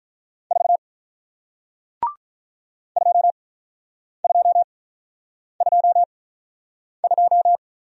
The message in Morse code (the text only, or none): 4 2 2 2 2